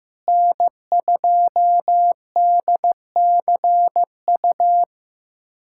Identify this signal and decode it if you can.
Morse code, heard as a keyed tone, about 15 wpm, 700 Hz. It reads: N2DCU
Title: USA Callsign DE Callsign - 5 Characters - 15wpm